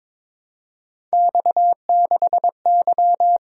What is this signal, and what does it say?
Morse code, heard as a keyed tone, about 26 wpm, 700 Hz. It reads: X6Y
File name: Speed-Racing - Three Letter-Number Combinations - 22wpm